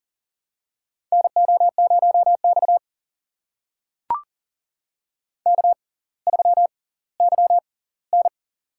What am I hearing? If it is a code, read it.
Morse code, heard as a keyed tone, about 39 wpm, spaced as for 20 wpm, 690 Hz. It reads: NO0X K3YN
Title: USA Call Signs - 4 Characters - 6x Character Spacing - 40wpm